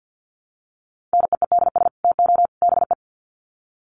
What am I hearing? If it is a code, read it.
Morse code, heard as a keyed tone, about 50 wpm, 700 Hz. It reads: NEEDS TO BE
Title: Sets of 3 Words 50wpm